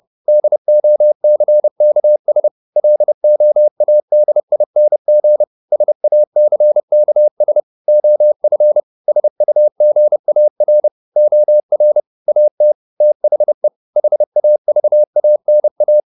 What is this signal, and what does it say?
Morse code, heard as a keyed tone, about 30 wpm, 600 Hz. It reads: DOCKS LOADING SACKS OF SUGAR OR AT THE HAVANA
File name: the old man and the sea 2 30wpm